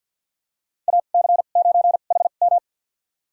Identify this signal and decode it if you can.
Morse code, heard as a keyed tone, about 50 wpm, spaced as for 36 wpm, 700 Hz. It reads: AC9HM
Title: USA Call Signs - 5 Characters - 2x Character Spacing - 50wpm